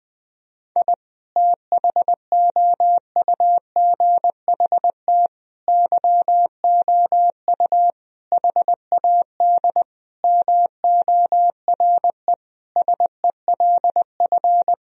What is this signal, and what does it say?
Morse code, heard as a keyed tone, about 20 wpm, 700 Hz. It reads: I THOUGHT YOU HAD MORE SELF
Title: Sentences from Top 1000 Words - Review Part 3 - 20wpm